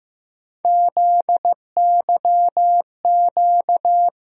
Morse code, heard as a keyed tone, about 15 wpm, 700 Hz. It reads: ZYQ
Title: USA Call Sign Postfixes - 3 Letters 15wpm